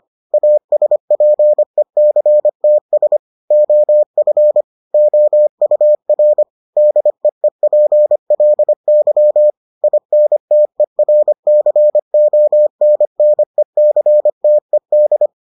Morse code, heard as a keyed tone, about 25 wpm, 600 Hz. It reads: ASPECTS OF OUR DEEPLY INTERCONNECTED